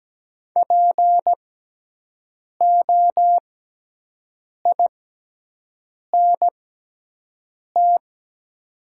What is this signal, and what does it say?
Morse code, heard as a keyed tone, about 17 wpm, spaced as for 6 wpm, 700 Hz. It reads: POINT
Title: 6x Character Spacing - Top 300 Words - 17wpm